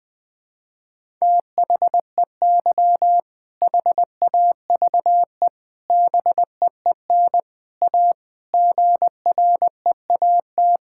Morse code, heard as a keyed tone, about 20 wpm, 700 Hz. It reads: THEY HAVE BEEN A GREAT
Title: Sentences with 3-Word Phrases - 20wpm